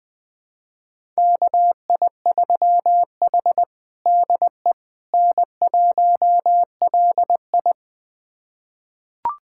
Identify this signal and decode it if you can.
Morse code, heard as a keyed tone, about 20 wpm, 700 Hz. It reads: KI3H DE N1LI
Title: USA Callsign DE Callsign - 4 Characters - 20wpm